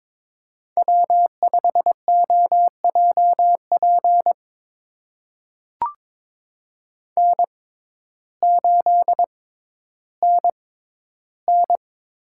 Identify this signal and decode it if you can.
Morse code, heard as a keyed tone, about 22 wpm, spaced as for 12 wpm, 700 Hz. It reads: W5OJP N8NN
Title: USA Call Signs - 5 Characters - 6x Character Spacing - 22wpm